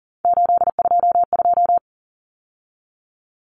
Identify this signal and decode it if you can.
Morse code, heard as a keyed tone, about 40 wpm, 700 Hz. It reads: 822
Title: Numbers - 3-Digit 40wpm